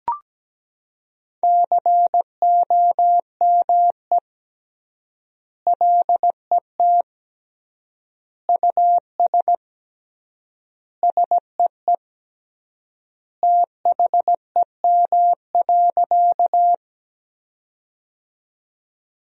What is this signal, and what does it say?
Morse code, heard as a keyed tone, about 17 wpm, 700 Hz. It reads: COME LET US SEE THEM.
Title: 3x Word-Spacing - Sentences from Top 300 Words - 17wpm